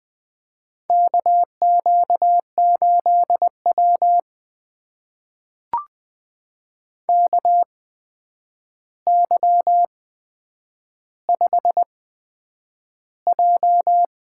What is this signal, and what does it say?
Morse code, heard as a keyed tone, about 20 wpm, spaced as for 9 wpm, 700 Hz. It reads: KQ8W KY5J
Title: USA Call Signs - 4 Characters - 8x Character Spacing - 20wpm